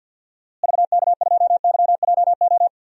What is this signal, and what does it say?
Morse code, heard as a keyed tone, about 52 wpm, 700 Hz. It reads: VK2YJO
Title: DXCC-100 Call Signs - 6 Characters - Repeated 2x - 50wpm